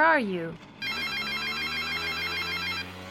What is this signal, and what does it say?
Speech. There is no code.